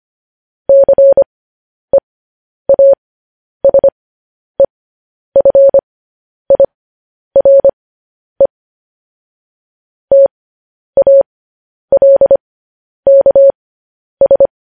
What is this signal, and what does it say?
Morse code, heard as a keyed tone, about 25 wpm, spaced as for 10 wpm, 550 Hz. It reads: CEASEFIRE TALKS